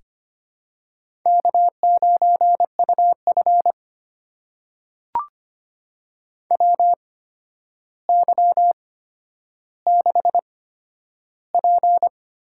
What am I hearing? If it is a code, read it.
Morse code, heard as a keyed tone, about 25 wpm, spaced as for 11 wpm, 700 Hz. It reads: K9UF WY6P
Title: USA Call Signs - 4 Characters - 8x Character Spacing - 25wpm